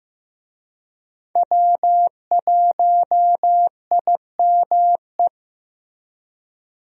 Morse code, heard as a keyed tone, about 15 wpm, 700 Hz. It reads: W1IME
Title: USA Call Signs - 5 Characters - Repeated 2x - 15wpm